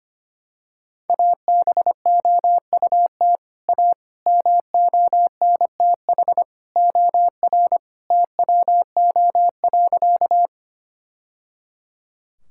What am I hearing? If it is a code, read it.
Morse code, heard as a keyed tone, about 25 wpm, 700 Hz. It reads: ABOUT A MONTH OR TWO.
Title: Sentences from Top 1000 Words - Review Part 5 - 25wpm